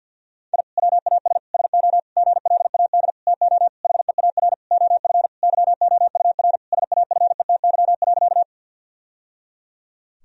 Morse code, heard as a keyed tone, about 46 wpm, 700 Hz. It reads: I WAS SO GLAD TO HEAR OF YOUR SAFETY.
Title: Sentences from Top 1000 Words - Review Part 4 - 50wpm